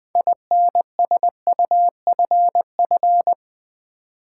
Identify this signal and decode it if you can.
Morse code, heard as a keyed tone, about 20 wpm, 700 Hz. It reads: INSUFF
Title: Build A Word - 6 Letters At A Time - 12-Letter Words - 20wpm